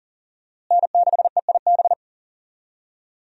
Morse code, heard as a keyed tone, about 37 wpm, 700 Hz. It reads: N6EIB
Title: USA Call Signs - 5 Characters - Repeated 3x - 40wpm